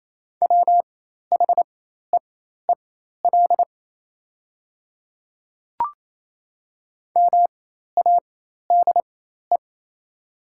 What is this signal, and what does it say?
Morse code, heard as a keyed tone, about 28 wpm, spaced as for 13 wpm, 700 Hz. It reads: WHEEL MADE
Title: 4x Character Spacing - Top 500 Words - 28wpm